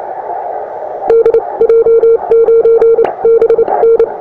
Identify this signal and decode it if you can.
Morse code, heard as a keyed tone, about 29 wpm, spaced as for 24 wpm, 460 Hz. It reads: DJ9BN
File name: attachments / europe 20m